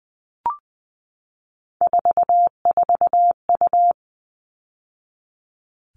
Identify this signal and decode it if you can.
Morse code, heard as a keyed tone, about 20 wpm, 700 Hz. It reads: 44U
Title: All Cut Numbers - 3-Digit 20wpm